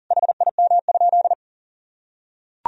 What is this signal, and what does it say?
Morse code, heard as a keyed tone, about 40 wpm, 700 Hz. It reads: HIM?
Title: Sentences from Top 1000 Words - Review Part 3 - 40wpm